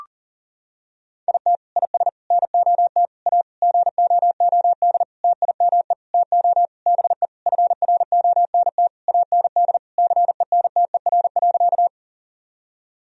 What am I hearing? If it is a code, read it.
Morse code, heard as a keyed tone, about 37 wpm, 700 Hz. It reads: IT IS NOT A GOOD TIME TO BE FRONT AND CENTER.
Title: Sentences from Top 1000 Words - Review Part 2 - 40wpm